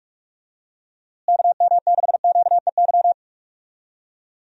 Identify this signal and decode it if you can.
Morse code, heard as a keyed tone, about 41 wpm, 700 Hz. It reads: KM6QEY
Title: USA Call Signs - 6 Characters - Repeated 2x - 45wpm